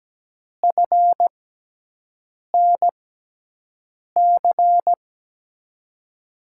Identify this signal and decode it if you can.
Morse code, heard as a keyed tone, about 17 wpm, spaced as for 6 wpm, 700 Hz. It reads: FNC
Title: USA Call Signs - 5 Characters - 6x Character Spacing - 17wpm